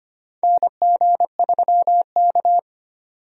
Morse code, heard as a keyed tone, about 25 wpm, 700 Hz. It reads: NG3K
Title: USA Call Signs - 4 Characters - 25wpm